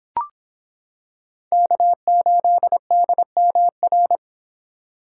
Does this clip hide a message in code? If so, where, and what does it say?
Yes: Morse code from 1.5 to 4.2 s, heard as a keyed tone, about 25 wpm, 700 Hz. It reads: K8DMR